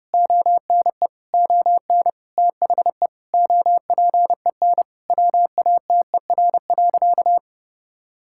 Morse code, heard as a keyed tone, about 29 wpm, 700 Hz. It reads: ONE ON THE OPEN WATER.
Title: Sentences from Top 1000 Words - Review Part 6 - 30wpm